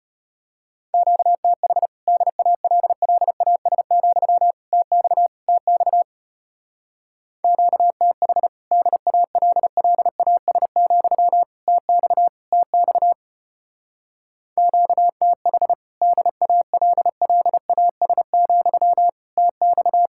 Morse code, heard as a keyed tone, about 33 wpm, 700 Hz. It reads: QTH DALLAS, TX TX QTH DALLAS, TX TX QTH DALLAS, TX